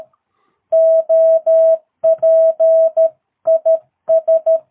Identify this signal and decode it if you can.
Morse code, heard as a keyed tone, about 13 wpm, 660 Hz. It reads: OP IS
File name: jh8cbh / 2022cw